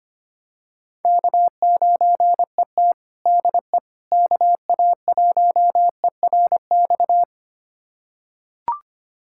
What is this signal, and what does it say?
Morse code, heard as a keyed tone, about 25 wpm, 700 Hz. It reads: K9ET DE KA1ERX
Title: USA Callsign DE Callsign 25wpm